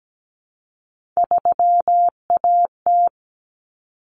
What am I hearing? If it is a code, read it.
Morse code, heard as a keyed tone, about 17 wpm, 700 Hz. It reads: 3AT
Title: Common Cut Numbers - 3-Digit 17wpm